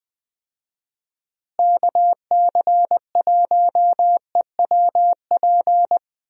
Morse code, heard as a keyed tone, about 20 wpm, 700 Hz. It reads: KC1EWP
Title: USA Call Signs - 6 Characters - Repeated 2x - 20wpm